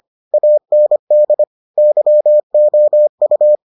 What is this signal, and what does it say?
Morse code, heard as a keyed tone, about 25 wpm, 600 Hz. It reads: AND YOU